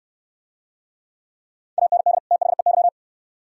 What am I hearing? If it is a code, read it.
Morse code, heard as a keyed tone, about 68 wpm, 690 Hz. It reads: AND THEY